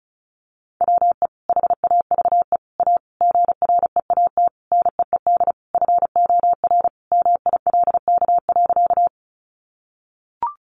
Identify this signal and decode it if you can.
Morse code, heard as a keyed tone, about 34 wpm, 700 Hz. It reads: WE HAVE A GREAT NEED FOR MILK.
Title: Sentences from Top 1500 Words 35wpm